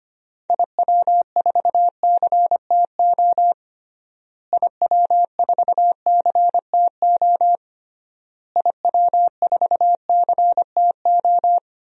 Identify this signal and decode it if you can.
Morse code, heard as a keyed tone, about 25 wpm, 700 Hz. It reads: IW4CTO IW4CTO IW4CTO
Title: DXCC-100 Call Signs - 6 Characters - Repeated 3x - 25wpm